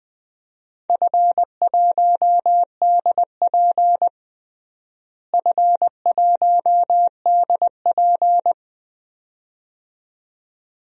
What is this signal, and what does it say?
Morse code, heard as a keyed tone, about 20 wpm, 700 Hz. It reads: F1DP F1DP
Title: DXCC-100 Call Signs - 4 Characters - Repeated 2x - 20wpm